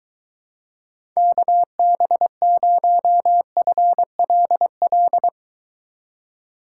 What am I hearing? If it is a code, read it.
Morse code, heard as a keyed tone, about 23 wpm, 700 Hz. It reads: KB0FLL